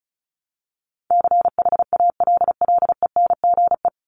Morse code, heard as a keyed tone, about 34 wpm, 700 Hz. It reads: CHALLENGE
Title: Top 751-1000 Words 35wpm